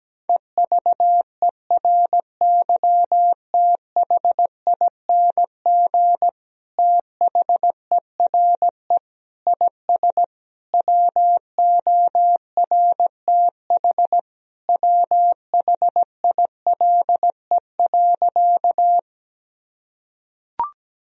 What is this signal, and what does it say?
Morse code, heard as a keyed tone, about 17 wpm, 700 Hz. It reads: EVERYTHING THERE IS WORTH WHILE.
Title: Sentences from Top 1000 Words - Review Part 1 - 17wpm